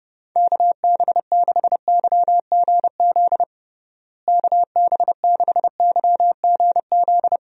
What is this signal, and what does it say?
Morse code, heard as a keyed tone, about 30 wpm, 700 Hz. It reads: KB6YGZ KB6YGZ
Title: USA Call Signs - 6 Characters - Repeated 2x - 30wpm